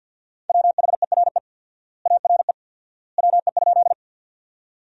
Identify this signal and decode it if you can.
Morse code, heard as a keyed tone, about 52 wpm, 700 Hz. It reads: WHERE ARE WE?
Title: 4x Word-Spacing - Sentences from Top 200 Words - 50wpm